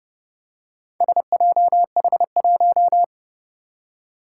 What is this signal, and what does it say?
Morse code, heard as a keyed tone, about 30 wpm, 700 Hz. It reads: SJH1
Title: Four Challenging Letter-Number Combinations 30wpm